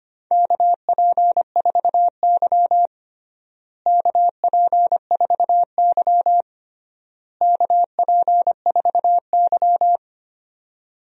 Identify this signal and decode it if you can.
Morse code, heard as a keyed tone, about 25 wpm, 700 Hz. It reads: KP4Y KP4Y KP4Y
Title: DXCC-100 Call Signs - 4 Characters - Repeated 3x - 25wpm